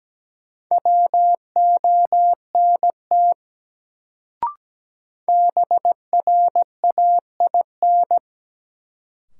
Morse code, heard as a keyed tone, about 17 wpm, 700 Hz. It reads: WONT BRAIN
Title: Sentences from Top 1000 Words - Review Part 6 - New Word Drill - 17wpm